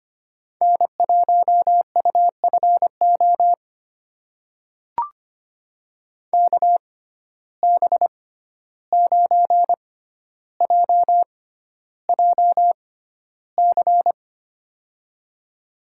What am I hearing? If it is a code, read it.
Morse code, heard as a keyed tone, about 25 wpm, spaced as for 12 wpm, 700 Hz. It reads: N1UFO KB9JJC
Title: USA Call Signs - 6x Character Spacing - 25wpm